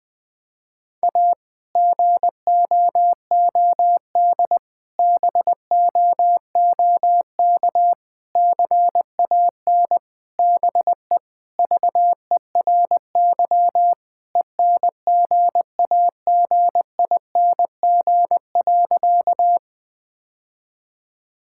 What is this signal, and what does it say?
Morse code, heard as a keyed tone, about 21 wpm, 700 Hz. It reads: A GOOD BOOK CAN BE VERY ENGAGING.
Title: Sentences with 2-Word Phrases - 20wpm